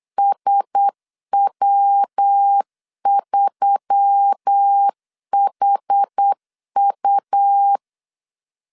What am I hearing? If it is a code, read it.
Morse code, heard as a keyed tone, about 8 wpm, 800 Hz. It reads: SW3HU
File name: LEZIONI / 6 PKX0 / 6 PK 8wpm 3003